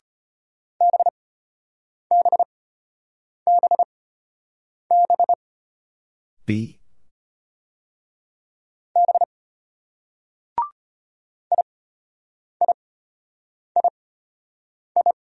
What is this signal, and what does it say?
Morse code, heard as a keyed tone, about 31 wpm, spaced as for 8 wpm, 700 Hz. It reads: BBBB B IIII
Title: Speed-Racing Single Letters 25wpm